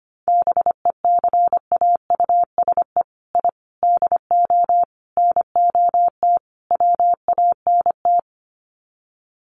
Morse code, heard as a keyed tone, about 25 wpm, 700 Hz. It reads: BECAUSE I DO NOT WANT
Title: Sets of 5 Words Encore 25wpm